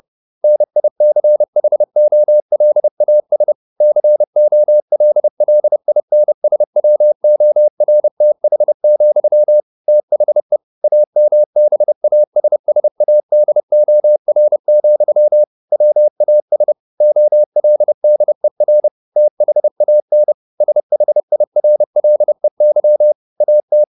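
Morse code, heard as a keyed tone, about 30 wpm, 600 Hz. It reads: NICHOLAS COLLINSWORTH, THE AMBASSADOR, WAS OLDER THAN SHIRLEY AT
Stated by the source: Morse code at 30 wpm, the novel Never by Ken Follett in English, from CW med Gnister.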